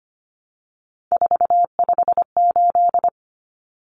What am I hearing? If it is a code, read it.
Morse code, heard as a keyed tone, about 25 wpm, 700 Hz. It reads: 458
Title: Numbers - 3-Digit 25wpm